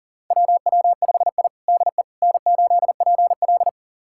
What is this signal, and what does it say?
Morse code, heard as a keyed tone, about 40 wpm, 700 Hz. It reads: WW5I DE N8PL
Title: USA Callsign DE Callsign - 4 Characters - 40wpm